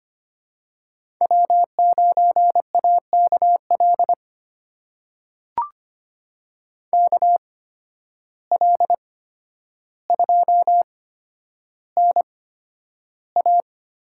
Morse code, heard as a keyed tone, about 25 wpm, spaced as for 11 wpm, 700 Hz. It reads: W9AKL KL2NA
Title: USA Call Signs - 5 Characters - 8x Character Spacing - 25wpm